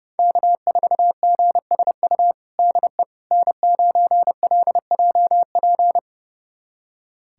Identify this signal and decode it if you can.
Morse code, heard as a keyed tone, about 30 wpm, 700 Hz. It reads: K4GSU DE N9LJP
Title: USA Callsign DE Callsign 30wpm